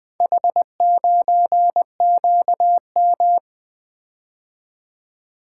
Morse code, heard as a keyed tone, about 20 wpm, 700 Hz. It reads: H9QM
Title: Four Letter-Number Combinations - 20wpm